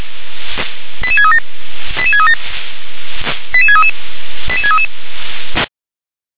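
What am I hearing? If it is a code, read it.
Five-tone selective call (ZVEI), heard as quick runs of five tones, tones of about 70 ms. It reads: 88637 88637 88633 88633
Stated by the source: ZVEI five-tone selective call 88631 received by SDR on channel path 357.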